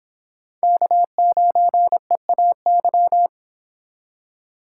Morse code, heard as a keyed tone, about 31 wpm, 700 Hz. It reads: K9EAY